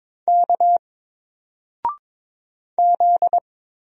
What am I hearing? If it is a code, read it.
Morse code, heard as a keyed tone, about 22 wpm, 700 Hz. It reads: K Z
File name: Single-Letter-Number-22wpm